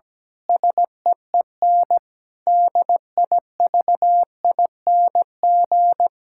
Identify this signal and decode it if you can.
Morse code, heard as a keyed tone, about 17 wpm, 700 Hz. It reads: SEEN DIVING
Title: Pangram Challenge v2 - 17wpm